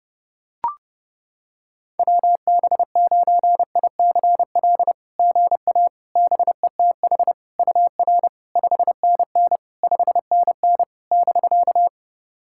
Morse code, heard as a keyed tone, about 28 wpm, 660 Hz. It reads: WB9ICL GA BETH UR 5NN 5NN <BK>
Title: SOTA QSO Practice 30wpm